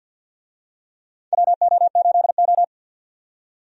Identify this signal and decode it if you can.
Morse code, heard as a keyed tone, about 49 wpm, 700 Hz. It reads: WO8O
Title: USA Call Signs - 4 Characters - 50wpm